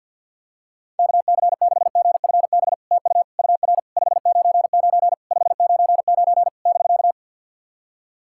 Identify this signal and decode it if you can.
Morse code, heard as a keyed tone, about 50 wpm, 700 Hz. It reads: KC6GFB TU UR 599 599 <BK>
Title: SOTA QSO Practice 50wpm